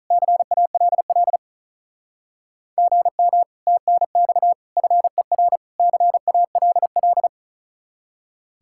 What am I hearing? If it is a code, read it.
Morse code, heard as a keyed tone, about 43 wpm, 700 Hz. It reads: CALL GM TNX FER CALL
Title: Speed-Racing Common QSO Phrases - 35wpm